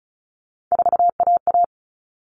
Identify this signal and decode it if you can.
Morse code, heard as a keyed tone, about 34 wpm, 700 Hz. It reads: 4AA
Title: All Cut Numbers - 3-Digit 35wpm